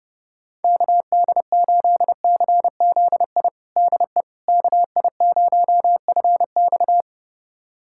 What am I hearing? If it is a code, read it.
Morse code, heard as a keyed tone, about 32 wpm, 700 Hz. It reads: KD8CZI DE KI0FX
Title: USA Callsign DE Callsign 30wpm